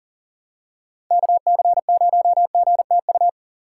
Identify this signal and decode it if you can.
Morse code, heard as a keyed tone, about 40 wpm, 700 Hz. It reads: KC0GTU
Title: USA Call Signs - 6 Characters - 2x Character Spacing - 40wpm